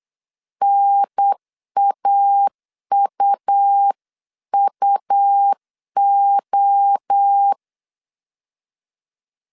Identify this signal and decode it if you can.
Morse code, heard as a keyed tone, about 8 wpm, 800 Hz. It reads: NAUUO
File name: CW / DET / 1 3 1500C S C 8wpm 4214